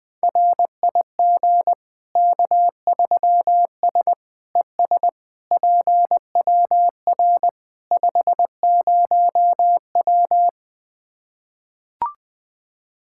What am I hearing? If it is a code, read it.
Morse code, heard as a keyed tone, about 20 wpm, 700 Hz. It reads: RIG K3S ES PWR 50W
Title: Practice QSO - Exchange 2 - Info - 20wpm